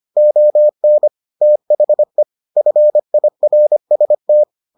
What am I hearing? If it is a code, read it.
Morse code, heard as a keyed tone, about 25 wpm, 600 Hz. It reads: ON THE FIRST